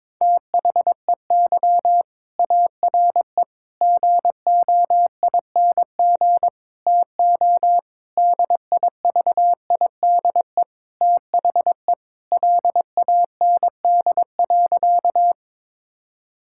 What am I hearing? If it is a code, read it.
Morse code, heard as a keyed tone, about 21 wpm, 700 Hz. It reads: THEY ARE GOING TO DIVIDE THE LAND.
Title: Sentences from Top 1600 Words 22wpm